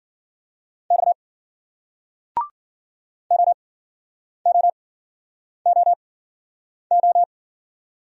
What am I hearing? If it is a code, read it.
Morse code, heard as a keyed tone, about 46 wpm, spaced as for 9 wpm, 700 Hz. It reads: X OOOO
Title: Speed-Racing Single Letters 40wpm